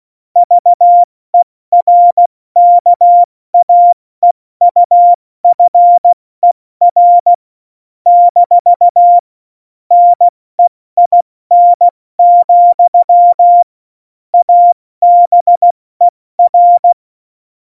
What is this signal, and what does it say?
Morse code, heard as a keyed tone, about 16 wpm, 700 Hz. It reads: VERKAEUFER - NEIN, ABER
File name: Morsekurs / Witze / Witz 14 April 2020, 70 F80